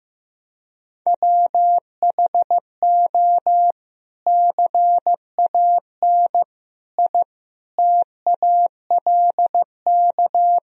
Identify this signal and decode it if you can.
Morse code, heard as a keyed tone, about 15 wpm, 700 Hz. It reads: WHO CAN I TALK
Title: Sentences from Top 1000 Words - Review Part 4 - 15wpm